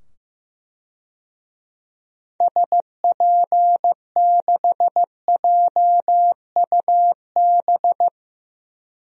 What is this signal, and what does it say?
Morse code, heard as a keyed tone, about 15 wpm, 700 Hz. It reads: SP6JUB
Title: DXCC-100 Call Signs - 6 Characters - 15wpm